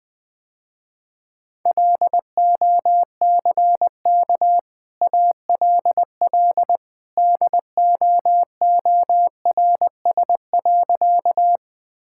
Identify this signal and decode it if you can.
Morse code, heard as a keyed tone, about 20 wpm, 700 Hz. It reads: LOCK ALL DOORS.
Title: Easy Sentences with Hints - 20wpm